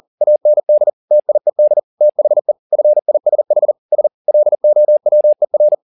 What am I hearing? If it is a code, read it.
Morse code, heard as a keyed tone, about 40 wpm, 600 Hz. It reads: AND TIED THE FISH S LOWER
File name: the old man and the sea 2 40wpm